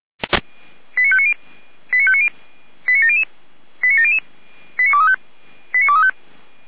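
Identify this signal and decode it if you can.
Five-tone selective call (ZVEI), heard as quick runs of five tones, tones of about 80 ms. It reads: 89590 89590 89700 89700 89235 89235
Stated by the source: ZVEI five-tone call received by SDR.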